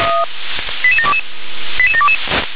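Five-tone selective call (ZVEI), heard as quick runs of five tones, tones of about 80 ms. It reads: 88722 88722
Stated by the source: Received on SDR as ZVEI five-tone selective call.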